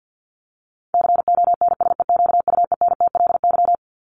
Morse code, heard as a keyed tone, about 50 wpm, 700 Hz. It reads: CONSEQUENTLY
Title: LY Suffix - Top 100 Words with Base Words 50wpm